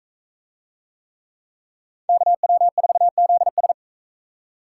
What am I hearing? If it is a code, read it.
Morse code, heard as a keyed tone, about 42 wpm, 700 Hz. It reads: KW4ZS